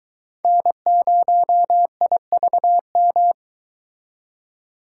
Morse code, heard as a keyed tone, about 22 wpm, 700 Hz. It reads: N0IVM